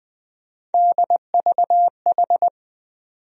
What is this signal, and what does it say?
Morse code, heard as a keyed tone, about 21 wpm, 700 Hz. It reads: DVH
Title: USA Call Sign Postfixes - 3 Letters 20wpm